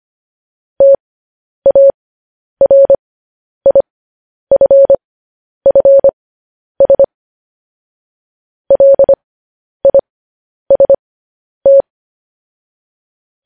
Morse code, heard as a keyed tone, about 25 wpm, spaced as for 10 wpm, 550 Hz. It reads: TARIFFS LIST
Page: news headlines in morse code at 10 wpm